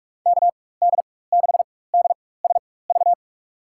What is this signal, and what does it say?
Morse code, heard as a keyed tone, about 41 wpm, spaced as for 20 wpm, 700 Hz. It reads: KD6DSV
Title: USA Call Signs - 6 Characters - 4x Character Spacing - 45wpm